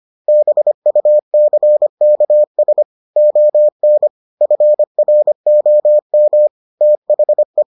Morse code, heard as a keyed tone, about 25 wpm, 600 Hz. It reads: BUCKS ON FROM THE